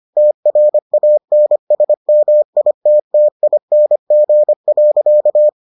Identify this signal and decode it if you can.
Morse code, heard as a keyed tone, about 25 wpm, 600 Hz. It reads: TRANSMITTING.